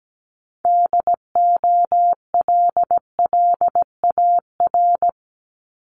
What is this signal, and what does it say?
Morse code, heard as a keyed tone, about 17 wpm, 700 Hz. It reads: DOLLAR
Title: Multi-letter Phonemes Repeated 3x 17wpm